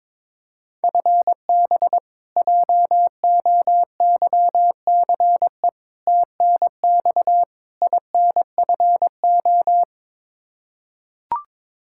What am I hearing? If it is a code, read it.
Morse code, heard as a keyed tone, about 21 wpm, 700 Hz. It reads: FB JOYCE TNX INFO
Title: K1USN SST QSO Practice 22wpm